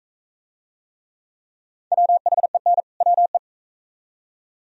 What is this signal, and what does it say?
Morse code, heard as a keyed tone, about 42 wpm, 700 Hz. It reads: WHEN WE